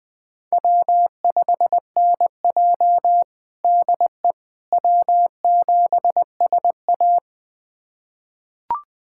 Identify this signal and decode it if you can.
Morse code, heard as a keyed tone, about 20 wpm, 700 Hz. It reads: W5NJ DE W7SA
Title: USA Callsign DE Callsign - 4 Characters - 20wpm